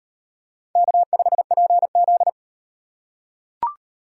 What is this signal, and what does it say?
Morse code, heard as a keyed tone, about 35 wpm, 700 Hz. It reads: K5PZ